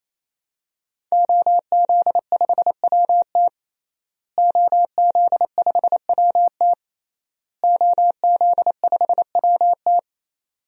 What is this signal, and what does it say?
Morse code, heard as a keyed tone, about 28 wpm, 700 Hz. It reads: OZ5WT OZ5WT OZ5WT
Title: DXCC-100 Call Signs - 5 Characters - Repeated 3x - 28wpm